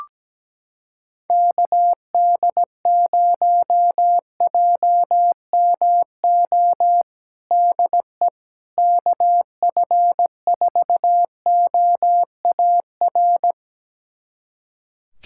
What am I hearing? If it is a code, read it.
Morse code, heard as a keyed tone, about 17 wpm, 700 Hz. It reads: KD0JMO DE KF4OAR
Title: USA Callsign DE Callsign - 6 Characters - 17wpm